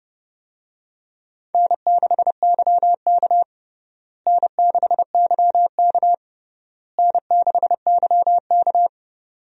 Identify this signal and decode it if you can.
Morse code, heard as a keyed tone, about 32 wpm, 700 Hz. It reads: N6YK N6YK N6YK
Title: USA Call Signs - 4 Characters - Repeated 3x - 30wpm